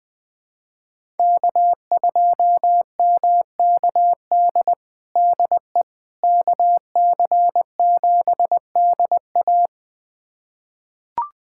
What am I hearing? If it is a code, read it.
Morse code, heard as a keyed tone, about 21 wpm, 700 Hz. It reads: K2MKD DE KC7DA
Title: USA Callsign DE Callsign - 5 Characters - 20wpm